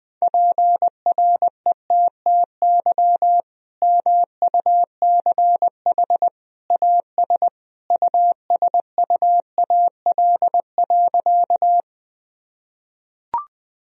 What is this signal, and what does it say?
Morse code, heard as a keyed tone, about 21 wpm, 700 Hz. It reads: PRETTY MUCH AS USUAL.
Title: Sentences from Top 1000 Words - Review Part 2 - 20wpm